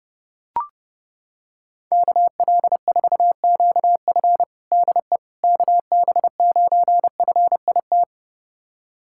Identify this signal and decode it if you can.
Morse code, heard as a keyed tone, about 32 wpm, 700 Hz. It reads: KL4QF DE KB9FIT
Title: USA Callsign DE Callsign 30wpm